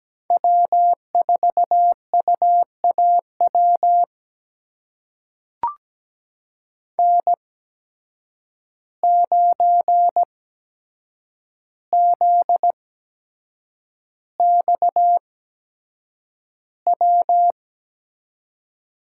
Morse code, heard as a keyed tone, about 17 wpm, spaced as for 7 wpm, 700 Hz. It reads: W4UAW N9ZXW
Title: USA Call Signs - 5 Characters - 8x Character Spacing - 17wpm